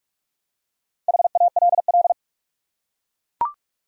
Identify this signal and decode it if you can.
Morse code, heard as a keyed tone, about 41 wpm, 700 Hz. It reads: HALL
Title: Four Letter Words - 45wpm